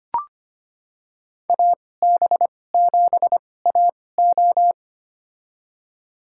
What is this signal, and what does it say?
Morse code, heard as a keyed tone, about 25 wpm, spaced as for 18 wpm, 700 Hz. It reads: AB7AO
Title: USA Call Signs - 5 Characters - 2x Character Spacing - 25wpm